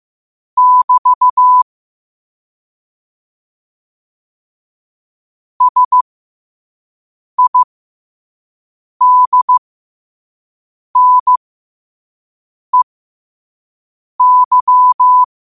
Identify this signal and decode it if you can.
Morse code, heard as a keyed tone, about 15 wpm, spaced as for 5 wpm, 1000 Hz. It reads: <BT> SIDNEY
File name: qotdfiles / 5wpm